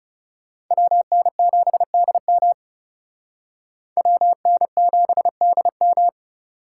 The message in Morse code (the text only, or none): WN7DM WN7DM